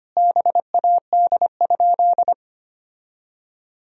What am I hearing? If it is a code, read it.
Morse code, heard as a keyed tone, about 25 wpm, 700 Hz. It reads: BAD?
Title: Sentences from Top 1000 Words - Review Part 1 - 25wpm